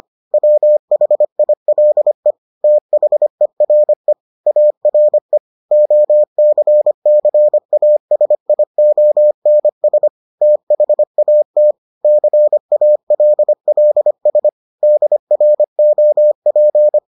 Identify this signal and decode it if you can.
Morse code, heard as a keyed tone, about 25 wpm, 600 Hz. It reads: WHILE THERE ARE OCCASIONS THAT CALLS DROP